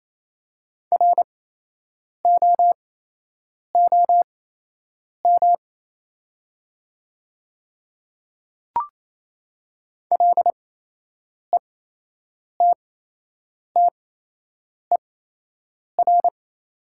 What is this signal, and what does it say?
Morse code, heard as a keyed tone, about 28 wpm, spaced as for 8 wpm, 690 Hz. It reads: ROOM LETTER
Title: 8x Character Spacing - Top 300 Words - 28wpm